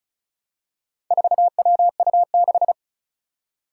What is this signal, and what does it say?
Morse code, heard as a keyed tone, about 36 wpm, 700 Hz. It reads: 4WU6
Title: Four Challenging Letter-Number Combinations 35wpm